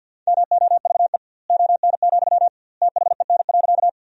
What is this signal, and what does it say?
Morse code, heard as a keyed tone, about 50 wpm, 700 Hz. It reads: MOVE ON, THEN.
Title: 2x Word-Spacing - Sentences from Top 200 Words - 50wpm